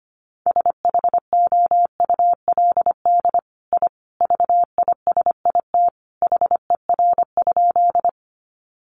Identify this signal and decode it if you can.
Morse code, heard as a keyed tone, about 25 wpm, 700 Hz. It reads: SHOULD I VISIT HER?
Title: Sentences from Top 900 Words 25wpm